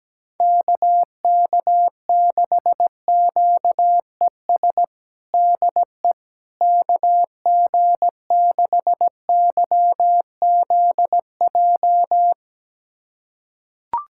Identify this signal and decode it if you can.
Morse code, heard as a keyed tone, about 17 wpm, 700 Hz. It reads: KK6QES DE KG6YZJ
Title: USA Callsign DE Callsign - 6 Characters - 17wpm